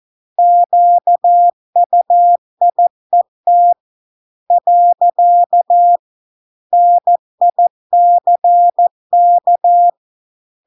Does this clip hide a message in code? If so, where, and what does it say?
Yes: Morse code from 0.4 to 9.9 s, heard as a keyed tone, about 14 wpm, 700 Hz. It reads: QUIET . NICK